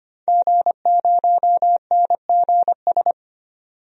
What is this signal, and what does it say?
Morse code, heard as a keyed tone, about 25 wpm, 700 Hz. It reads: G0NGS
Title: DXCC-100 Call Signs - 5 Characters - 25wpm